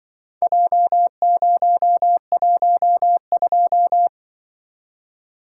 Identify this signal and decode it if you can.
Morse code, heard as a keyed tone, about 24 wpm, 700 Hz. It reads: J012
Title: Speed-Racing - Four Letter-Number Combinations - 20wpm